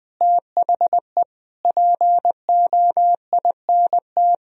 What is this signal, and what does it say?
Morse code, heard as a keyed tone, about 20 wpm, 700 Hz. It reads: THE POINT
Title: Speed-Racing - Sets of 5 Words 17wpm